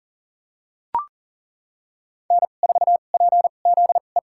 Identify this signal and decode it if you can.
Morse code, heard as a keyed tone, about 37 wpm, spaced as for 28 wpm, 700 Hz. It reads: N4PZE
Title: USA Call Signs - 2x Character Spacing - 40wpm